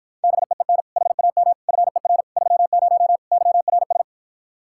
Morse code, heard as a keyed tone, about 53 wpm, 700 Hz. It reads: BEEN HAM FER 30 YRS